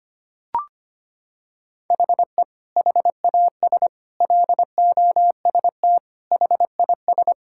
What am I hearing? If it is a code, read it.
Morse code, heard as a keyed tone, about 25 wpm, 700 Hz. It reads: HE HAS LOST HIS
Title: Sentences from Top 1600 Words 25wpm